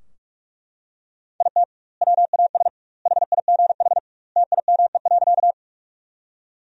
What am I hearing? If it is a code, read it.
Morse code, heard as a keyed tone, about 41 wpm, 700 Hz. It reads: IT WAS HIGH TIME.
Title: 2x Word-Spacing - Sentences from Top 200 Words - 45wpm